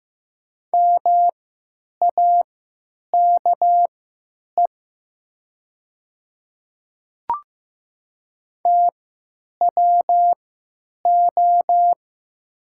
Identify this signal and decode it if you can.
Morse code, heard as a keyed tone, about 15 wpm, spaced as for 9 wpm, 690 Hz. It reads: MAKE TWO
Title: 3x Character Spacing - Top 100 Words - 15wpm